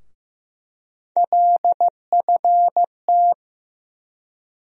Morse code, heard as a keyed tone, about 15 wpm, 700 Hz. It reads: LFT
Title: USA Call Sign Postfixes - 3 Letters 15wpm